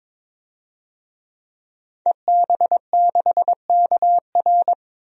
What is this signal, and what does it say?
Morse code, heard as a keyed tone, about 22 wpm, 700 Hz. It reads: EB6KR
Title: DXCC-100 Call Signs - 5 Characters - 22wpm